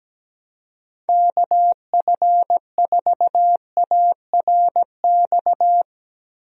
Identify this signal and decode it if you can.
Morse code, heard as a keyed tone, about 17 wpm, 700 Hz. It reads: KF4ARX
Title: USA Call Signs - 6 Characters - Repeated 2x - 17wpm